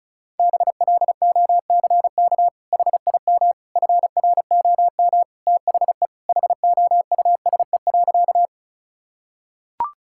Morse code, heard as a keyed tone, about 36 wpm, 700 Hz. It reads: BLOCK HIM FROM THE HOUSE.
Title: Sentences from Top 1600 Words 35wpm